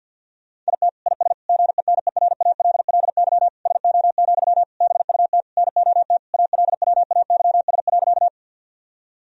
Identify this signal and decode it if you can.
Morse code, heard as a keyed tone, about 52 wpm, 700 Hz. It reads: IT IS GENERALLY SO, BUT NOT ALWAYS.
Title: Sentences from Top 1000 Words - Review Part 4 - 50wpm